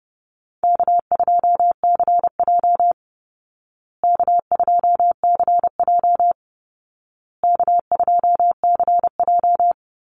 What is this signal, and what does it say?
Morse code, heard as a keyed tone, about 30 wpm, 700 Hz. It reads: K2CJ K2CJ K2CJ
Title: Call Signs Repeated 2x - 30wpm